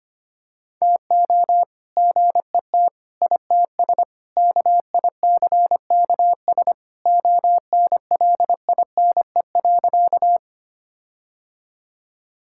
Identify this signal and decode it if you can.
Morse code, heard as a keyed tone, about 25 wpm, 700 Hz. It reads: TO GET ITS KICKS ONLINE.
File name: practice / 25wpm / All-Age Appropriate Jokes - 25wpm